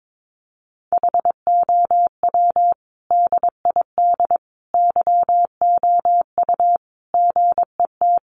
Morse code, heard as a keyed tone, about 22 wpm, 700 Hz. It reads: HOW DID YOU GET
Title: Sentences from Top 1100 Words 22wpm